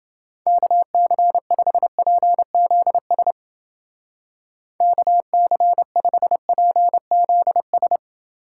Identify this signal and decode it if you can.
Morse code, heard as a keyed tone, about 29 wpm, 700 Hz. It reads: KC5PZS KC5PZS